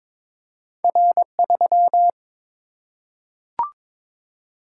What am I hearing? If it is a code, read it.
Morse code, heard as a keyed tone, about 22 wpm, 700 Hz. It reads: R3
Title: Single-Letter Single-Number 22wpm